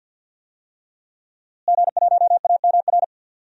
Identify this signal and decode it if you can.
Morse code, heard as a keyed tone, about 46 wpm, 700 Hz. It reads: G1AMR
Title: DXCC-100 Call Signs - 5 Characters - Repeated 3x - 50wpm